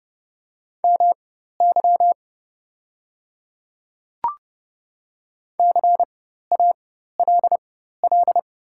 Morse code, heard as a keyed tone, about 30 wpm, spaced as for 14 wpm, 700 Hz. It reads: MY CALL
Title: 4x Character Spacing - Top 100 Words - 30wpm